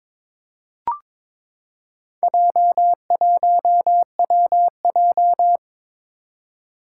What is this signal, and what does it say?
Morse code, heard as a keyed tone, about 22 wpm, 700 Hz. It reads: J1WJ
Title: Four Challenging Letter-Number Combinations 22wpm